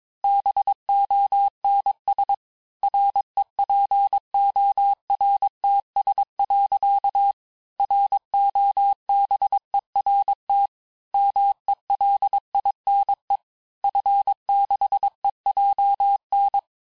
Morse code, heard as a keyed tone, about 22 wpm, 800 Hz. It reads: BONS REPORTS. ROBERT MELINE F6EJN
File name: cours-du-mercredi / audios / cours-nr-860-FIN-de-QSOr-vitesse-21MPM